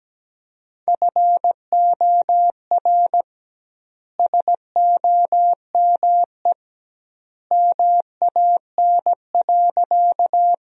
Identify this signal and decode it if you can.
Morse code, heard as a keyed tone, about 17 wpm, 700 Hz. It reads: FOR SOME MAN.
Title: 2x Word-Spacing - Sentences from Top 200 Words - 17wpm